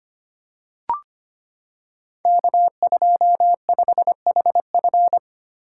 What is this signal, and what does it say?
Morse code, heard as a keyed tone, about 25 wpm, 700 Hz. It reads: K25HF
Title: Code Groups - 5 Letter:Numbers - 25wpm